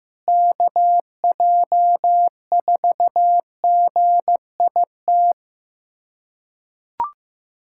Morse code, heard as a keyed tone, about 16 wpm, 700 Hz. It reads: KJ4GIT
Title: USA Callsign DE Callsign - 6 Characters - 15wpm